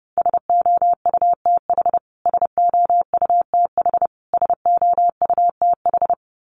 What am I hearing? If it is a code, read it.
Morse code, heard as a keyed tone, about 28 wpm, 700 Hz. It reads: SOUTH SOUTH SOUTH
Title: Top 500 Words Repeated 30wpm